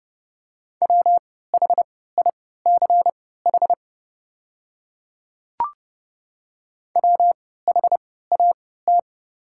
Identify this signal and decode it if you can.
Morse code, heard as a keyed tone, about 29 wpm, spaced as for 17 wpm, 690 Hz. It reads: WHICH WHAT
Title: 3x Character Spacing - Top 100 Words - 30wpm